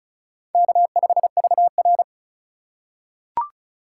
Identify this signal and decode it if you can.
Morse code, heard as a keyed tone, about 36 wpm, 700 Hz. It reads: K5VR
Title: K1USN SST QSO Practice - 35wpm